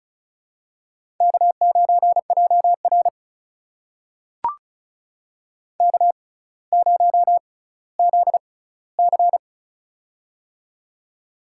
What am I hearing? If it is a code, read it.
Morse code, heard as a keyed tone, about 36 wpm, spaced as for 18 wpm, 700 Hz. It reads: K9JR K0ZC
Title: USA Call Signs - 4 Characters - 6x Character Spacing - 35wpm